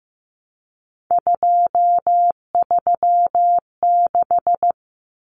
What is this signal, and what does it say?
Morse code, heard as a keyed tone, about 15 wpm, 700 Hz. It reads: 236
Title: Numbers - 3-Digit 15wpm